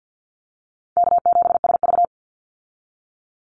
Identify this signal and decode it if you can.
Morse code, heard as a keyed tone, about 50 wpm, 700 Hz. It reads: K7SV
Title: Easy Call Signs - 50wpm